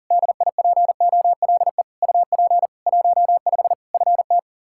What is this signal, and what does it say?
Morse code, heard as a keyed tone, about 40 wpm, 700 Hz. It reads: DIPOLE UP 15 FT
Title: Practice QSO - Exchange 2 - Info - 40wpm